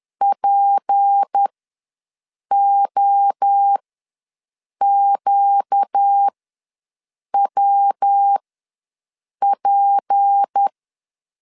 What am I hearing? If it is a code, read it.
Morse code, heard as a keyed tone, about 10 wpm, spaced as for 6 wpm, 800 Hz. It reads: POQWP
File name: LEZIONI / 6 PKX0 / 6 P ws 10wpm 3751